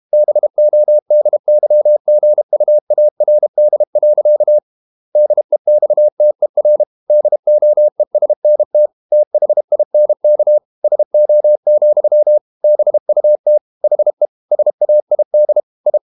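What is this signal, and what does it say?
Morse code, heard as a keyed tone, about 32 wpm, 600 Hz. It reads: BODYGUARD. DEXTER DOESNT THINK SO, BUT HE SAID I
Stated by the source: Morse code practice at 32 wpm from CW med Gnister, reading Ken Follett's novel Never in English.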